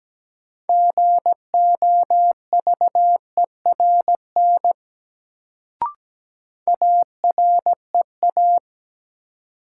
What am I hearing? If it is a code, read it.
Morse code, heard as a keyed tone, about 17 wpm, 700 Hz. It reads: GOVERN AREA
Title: Sentences from Top 1000 Words - Review Part 2 - New Word Drill - 17wpm